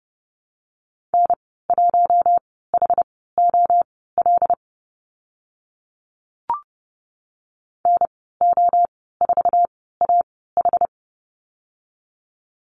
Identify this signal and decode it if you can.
Morse code, heard as a keyed tone, about 30 wpm, spaced as for 17 wpm, 700 Hz. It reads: N1HOL NO4AH
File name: Lesson 14g - Number 2 - Call Signs 30wpm